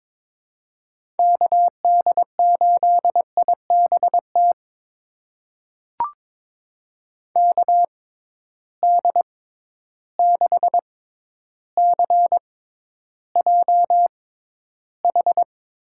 Morse code, heard as a keyed tone, about 22 wpm, spaced as for 11 wpm, 700 Hz. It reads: KD8IBT KD6CJH
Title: USA Call Signs - 6 Characters - 6x Character Spacing - 22wpm